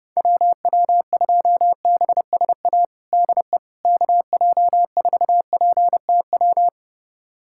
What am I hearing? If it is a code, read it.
Morse code, heard as a keyed tone, about 32 wpm, 700 Hz. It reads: WW2BSA DE KJ4PTW
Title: USA Callsign DE Callsign - 6 Characters - 30wpm